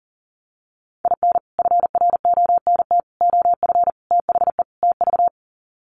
Morse code, heard as a keyed tone, about 40 wpm, 700 Hz. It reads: IN FRONT OF THE TV
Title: Sets of 5 Words Encore2 40wpm